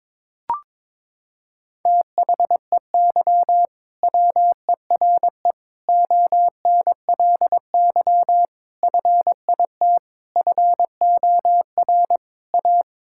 Morse code, heard as a keyed tone, about 22 wpm, 700 Hz. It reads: THEY WERE ONLY FIT FOR A
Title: Sentences from Top 1000 Words - Review Part 5 - 22wpm